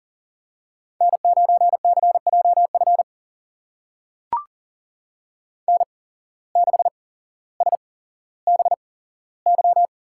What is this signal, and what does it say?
Morse code, heard as a keyed tone, about 39 wpm, spaced as for 17 wpm, 700 Hz. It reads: N9CJF N6SBY
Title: USA Call Signs - 5 Characters - 8x Character Spacing - 40wpm